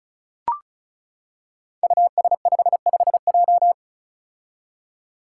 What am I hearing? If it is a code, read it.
Morse code, heard as a keyed tone, about 36 wpm, 700 Hz. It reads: US55J
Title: Five Challenging Letter-Number Combinations 35wpm